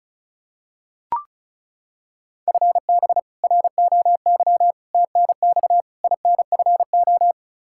Morse code, heard as a keyed tone, about 36 wpm, 700 Hz. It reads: FB ROY TNX INFO
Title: K1USN SST QSO Practice 35wpm